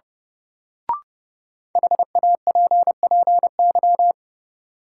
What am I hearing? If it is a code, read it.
Morse code, heard as a keyed tone, about 30 wpm, 700 Hz. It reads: HAPPY
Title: Sentences from Top 1000 Words - New Word Drill - 30wpm